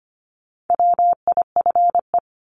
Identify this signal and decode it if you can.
Morse code, heard as a keyed tone, about 25 wpm, 700 Hz. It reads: WIFE
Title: Top 1751-2000 Words 25wpm